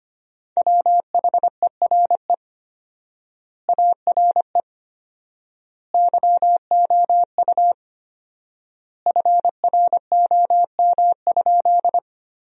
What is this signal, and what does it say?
Morse code, heard as a keyed tone, about 25 wpm, 700 Hz. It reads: WHERE ARE YOU FROM?
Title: 4x Word-Spacing - Sentences from Top 200 Words - 25wpm